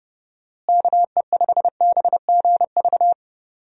Morse code, heard as a keyed tone, about 29 wpm, 700 Hz. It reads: KE5BGV
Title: USA Call Signs - 6 Characters - Repeated 2x - 30wpm